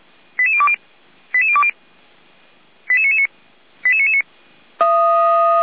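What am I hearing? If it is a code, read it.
Five-tone selective call (ZVEI), heard as quick runs of five tones, tones of about 80 ms. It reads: 88020 88020 88999 88999
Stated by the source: ZVEI five-tone call received by SDR, from the Feuerwehr Orlamünde audio server.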